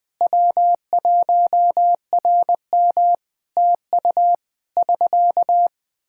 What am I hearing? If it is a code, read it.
Morse code, heard as a keyed tone, about 21 wpm, 700 Hz. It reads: W1RM TU <SK>
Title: Practice QSO - Exchange 4 - Ending - 20wpm